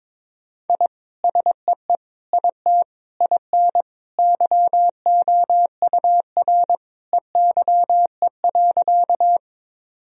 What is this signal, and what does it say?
Morse code, heard as a keyed tone, about 21 wpm, 700 Hz. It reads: I SEE IT IN YOUR EYE.
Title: Sentences from Top 1000 Words - Review Part 1 - 22wpm